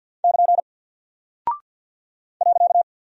Morse code, heard as a keyed tone, about 48 wpm, 700 Hz. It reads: <KN> .
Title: Single Character (Letter:Number:Punctuation:Prosign) Alt 50wpm